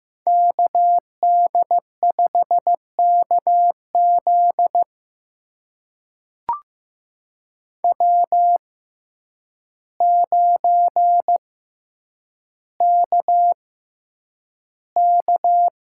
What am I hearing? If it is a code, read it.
Morse code, heard as a keyed tone, about 15 wpm, spaced as for 8 wpm, 700 Hz. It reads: KD5KZ W9KK
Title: USA Call Signs - 5 Characters - 6x Character Spacing - 15wpm